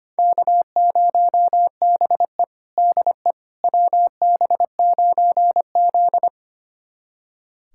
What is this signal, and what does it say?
Morse code, heard as a keyed tone, about 25 wpm, 700 Hz. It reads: K0BE DE WB9Z
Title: USA Callsign DE Callsign - 4 Characters - 25wpm